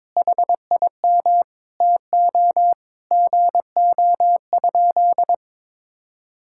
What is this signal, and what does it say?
Morse code, heard as a keyed tone, about 21 wpm, 700 Hz. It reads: HIM TO GO?
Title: Sentences from Top 1600 Words 22wpm